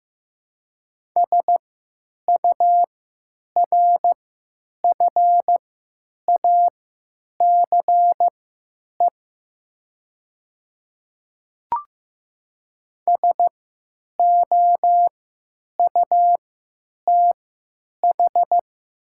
Morse code, heard as a keyed tone, about 15 wpm, spaced as for 8 wpm, 700 Hz. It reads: SURFACE SOUTH
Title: 3x Character Spacing - Top 500 Words - 15wpm